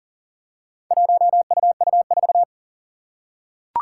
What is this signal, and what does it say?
Morse code, heard as a keyed tone, about 40 wpm, 700 Hz. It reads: 1UU4
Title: Four Challenging Letter-Number Combinations 40wpm